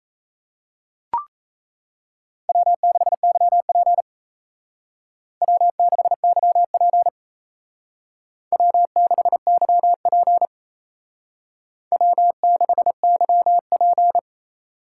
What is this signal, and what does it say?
Morse code, heard as a keyed tone, about 36 wpm, 700 Hz. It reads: W6YP W6YP W6YP W6YP